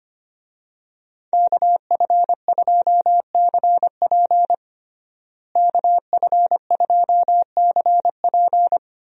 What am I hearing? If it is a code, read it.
Morse code, heard as a keyed tone, about 25 wpm, 700 Hz. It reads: KF2CP KF2CP
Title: USA Call Signs - 5 Characters - Repeated 3x - 25wpm